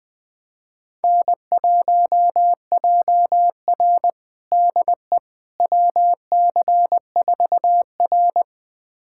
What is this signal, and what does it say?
Morse code, heard as a keyed tone, about 20 wpm, 700 Hz. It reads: N1JR DE WC4R
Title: USA Callsign DE Callsign - 4 Characters - 20wpm